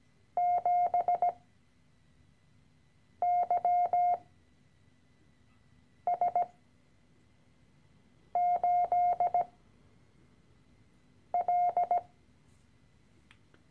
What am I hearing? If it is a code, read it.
Morse code, heard as a keyed tone, about 17 wpm, spaced as for 4 wpm, 710 Hz. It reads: 7YS8L
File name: CW-Online-Kurs / Anfänger / Lektion 013